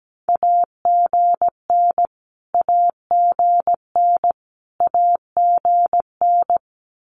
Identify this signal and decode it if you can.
Morse code, heard as a keyed tone, about 17 wpm, 700 Hz. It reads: AGN AGN AGN
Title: Abbreviations Repeated 3x - 17wpm